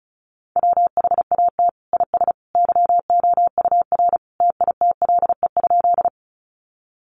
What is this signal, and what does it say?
Morse code, heard as a keyed tone, about 34 wpm, 700 Hz. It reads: WHAT IS YOUR TITLE?
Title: Sentences from Top 1200 Words 35wpm